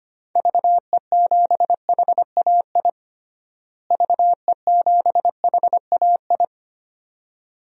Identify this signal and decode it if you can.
Morse code, heard as a keyed tone, about 25 wpm, 700 Hz. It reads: VE7HAI VE7HAI
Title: DXCC-100 Call Signs - 6 Characters - Repeated 2x - 25wpm